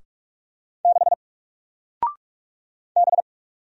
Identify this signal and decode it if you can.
Morse code, heard as a keyed tone, about 44 wpm, 700 Hz. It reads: 6 B
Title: Single Character (Letter:Number:Punctuation:Prosign) Alt 45wpm